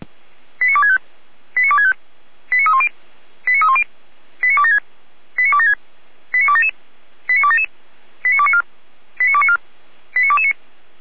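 Five-tone selective call (ZVEI), heard as quick runs of five tones, tones of about 80 ms. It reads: 89265 89265 89310 89310 89276 89276 89277 89277 89294 89294 89229
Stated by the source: ZVEI five-tone call 89240 received by SDR, from the feuerwehr-orlamuende.org fire brigade server.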